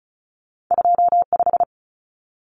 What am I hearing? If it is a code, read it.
Morse code, heard as a keyed tone, about 36 wpm, 700 Hz. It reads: 25
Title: Common Cut Numbers - 2-Digit 35wpm